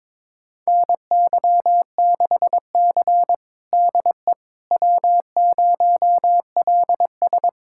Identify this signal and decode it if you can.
Morse code, heard as a keyed tone, about 22 wpm, 700 Hz. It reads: NY6C DE W0LS
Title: USA Callsign DE Callsign - 4 Characters - 22wpm